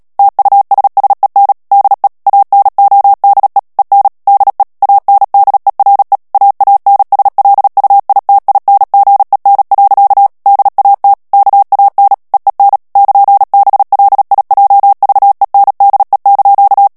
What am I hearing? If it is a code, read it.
Morse code, heard as a keyed tone, about 37 wpm, 800 Hz. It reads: TUSSEN DE ANODE EN DE ANDERE AANSLUITINGEN. DAT KAN EEN <KN>BLIJVENDE)
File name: morse mp3 / HOLLANDAIS texte clair 2 vitesse 35 mots minute